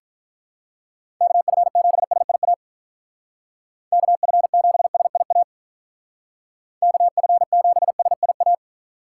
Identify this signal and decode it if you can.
Morse code, heard as a keyed tone, about 44 wpm, 700 Hz. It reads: KF7SIA KF7SIA KF7SIA
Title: Speed Racing - USA 2x3 Call Signs 35wpm